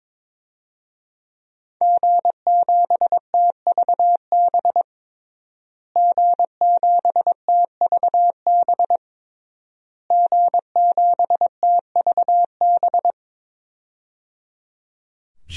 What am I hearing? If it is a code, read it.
Morse code, heard as a keyed tone, about 22 wpm, 700 Hz. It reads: G7TVB G7TVB G7TVB
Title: DXCC-100 Call Signs - 5 Characters - Repeated 3x - 22wpm